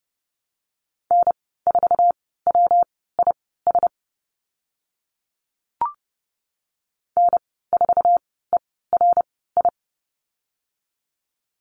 Morse code, heard as a keyed tone, about 31 wpm, spaced as for 17 wpm, 700 Hz. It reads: N4WIS N4ERI
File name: Lesson 19g - Letter w - Call Signs 30wpm